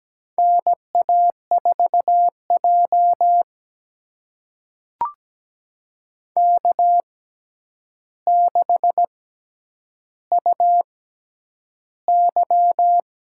Morse code, heard as a keyed tone, about 17 wpm, spaced as for 9 wpm, 700 Hz. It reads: NA4J K6UY
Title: USA Call Signs - 4 Characters - 6x Character Spacing - 17wpm